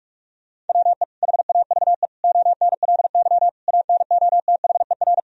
Morse code, heard as a keyed tone, about 41 wpm, 700 Hz. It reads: WE HAVE ONLY ANOTHER
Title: Sentences from Top 1000 Words - Review Part 1 - 45wpm